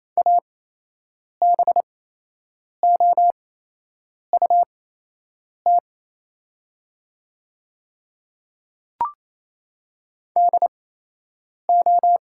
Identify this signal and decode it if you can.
Morse code, heard as a keyed tone, about 28 wpm, spaced as for 8 wpm, 700 Hz. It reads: ABOUT DO
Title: 8x Character Spacing - Top 100 Words - 28wpm